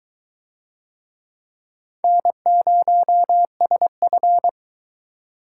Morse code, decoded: N0SF

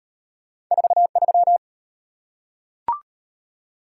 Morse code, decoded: 43

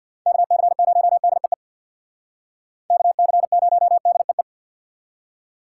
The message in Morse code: KC0DEE KC0DEE

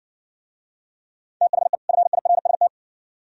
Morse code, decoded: THE FIRST

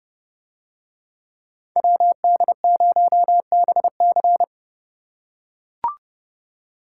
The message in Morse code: WD0BC

